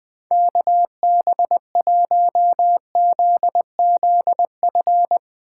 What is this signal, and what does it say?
Morse code, heard as a keyed tone, about 21 wpm, 700 Hz. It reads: KB1ZZF
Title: USA Call Signs - 6 Characters - 2x Character Spacing - 20wpm